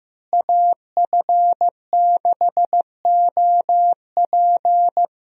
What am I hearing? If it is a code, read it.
Morse code, heard as a keyed tone, about 15 wpm, 700 Hz. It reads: AF6OP
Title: USA Call Signs - 5 Characters - Repeated 2x - 15wpm